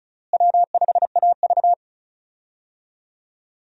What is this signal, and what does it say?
Morse code, heard as a keyed tone, about 34 wpm, 700 Hz. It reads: W5AV
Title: USA Call Signs - 4 Characters - Repeated 3x - 35wpm